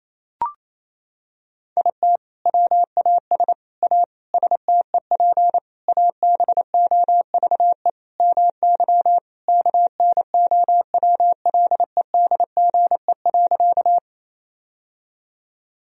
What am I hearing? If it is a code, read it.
Morse code, heard as a keyed tone, about 28 wpm, 700 Hz. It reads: IT WAS A STEP ABOVE MY KNOWLEDGE.
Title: Sentences from Top 1000 Words - Review Part 3 - 28wpm